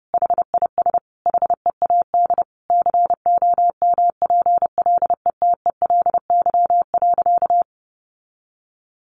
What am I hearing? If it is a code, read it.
Morse code, heard as a keyed tone, about 30 wpm, 700 Hz. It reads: HIS HEAD COMPLETELY.
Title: Sentences from Top 1400 Words 30wpm